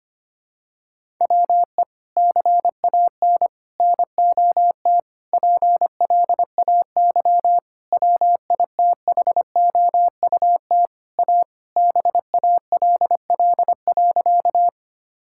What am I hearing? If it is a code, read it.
Morse code, heard as a keyed tone, about 25 wpm, 700 Hz. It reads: WE CAN NOT PLAY WITHOUT A BALL.
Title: Sentences from Top 1000 Words - Review Part 6 - 25wpm